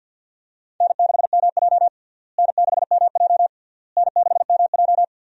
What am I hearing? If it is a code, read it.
Morse code, heard as a keyed tone, about 47 wpm, 700 Hz. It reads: N6MJ N6MJ N6MJ
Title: USA Call Signs - 4 Characters - Repeated 3x - 50wpm